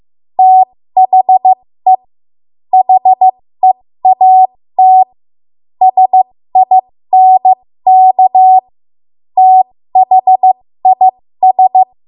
Morse code, heard as a keyed tone, about 15 wpm, spaced as for 13 wpm, 750 Hz. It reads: THE HEAT SINK THIS
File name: Morse / 13WPM